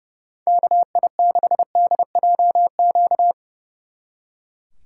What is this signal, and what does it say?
Morse code, heard as a keyed tone, about 30 wpm, 700 Hz. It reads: KI6DJQ
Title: Speed Racing - USA 2x3 Call Signs 30wpm